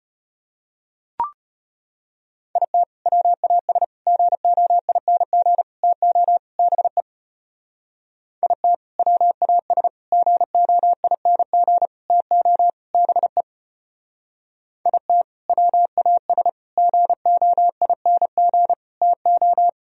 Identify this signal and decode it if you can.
Morse code, heard as a keyed tone, about 31 wpm, 700 Hz. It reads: IT WAS GOING TO BE IT WAS GOING TO BE IT WAS GOING TO
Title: Speed-Racing - Sets of 5 Words 25wpm